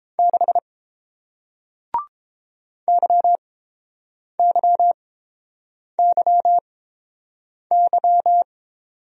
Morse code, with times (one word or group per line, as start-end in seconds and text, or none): 0.2-0.6 6
2.9-8.4 YYYY